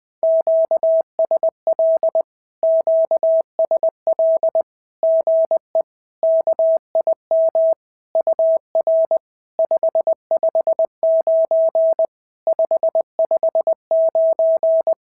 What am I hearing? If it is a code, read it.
Morse code, heard as a keyed tone, about 20 wpm, 650 Hz. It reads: QSL QSL GE KIM UR 559 559
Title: SOTA QSO Practice 20wpm